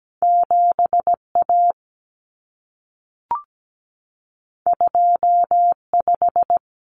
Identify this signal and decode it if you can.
Morse code, heard as a keyed tone, about 17 wpm, 700 Hz. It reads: 7A 25
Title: Common Cut Numbers - 2-Digit 17wpm